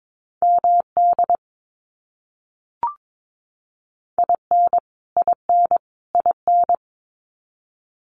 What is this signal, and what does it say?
Morse code, heard as a keyed tone, about 22 wpm, 700 Hz. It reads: MD IN IN IN
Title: US States Repeated 3x 22wpm